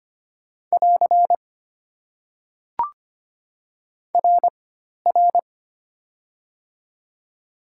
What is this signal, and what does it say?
Morse code, heard as a keyed tone, about 25 wpm, 700 Hz. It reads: <AR> R R
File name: Top 90 QSO Elements - 4x Character Spacing - 25wpm